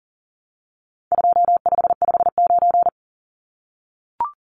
Morse code, heard as a keyed tone, about 40 wpm, 700 Hz. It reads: 2559
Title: Numbers - 4-Digit 40wpm